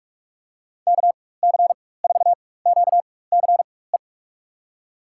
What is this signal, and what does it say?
Morse code, heard as a keyed tone, about 41 wpm, spaced as for 20 wpm, 700 Hz. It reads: KC4QCE
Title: USA Call Signs - 6 Characters - 4x Character Spacing - 45wpm